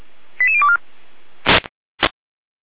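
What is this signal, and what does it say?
Five-tone selective call (ZVEI), heard as quick runs of five tones, tones of about 70 ms. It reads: 88024
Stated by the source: Received by SDR, archived by feuerwehr-orlamuende.org (fire brigade Orlamünde).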